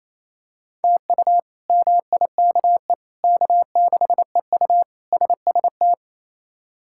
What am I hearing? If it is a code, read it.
Morse code, heard as a keyed tone, about 28 wpm, 700 Hz. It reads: TU MIKE K6EU SST
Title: K1USN SST QSO Practice - 28wpm